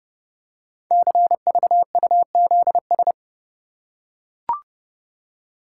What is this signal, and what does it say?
Morse code, heard as a keyed tone, about 30 wpm, 700 Hz. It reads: CVUZS
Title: Code Groups - 5 Letters - 30wpm